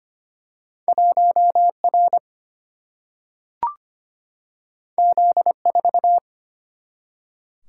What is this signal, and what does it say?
Morse code, heard as a keyed tone, about 25 wpm, 700 Hz. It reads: 1R Z4
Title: Single-Letter Single-Number 25wpm